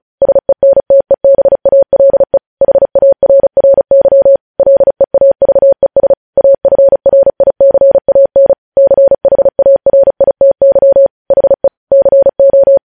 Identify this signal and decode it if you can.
Morse code, heard as a keyed tone, about 36 wpm, 550 Hz. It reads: SENTEBALE HARRY LEAVES AFRICAN CHARITY HE CO
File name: podcast / News headlines-1x-35x35